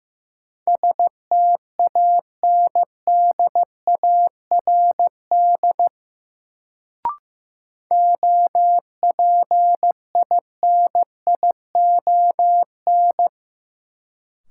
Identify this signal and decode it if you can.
Morse code, heard as a keyed tone, about 15 wpm, 700 Hz. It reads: STANDARD OPINION
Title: Sentences from Top 1200 Words - New Word Drill Alt - 15wpm